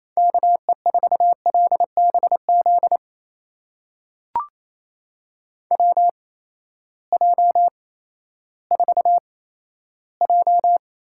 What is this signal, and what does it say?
Morse code, heard as a keyed tone, about 28 wpm, spaced as for 14 wpm, 700 Hz. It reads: KE4LBZ WJ4J
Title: USA Call Signs - 8x Character Spacing - 28wpm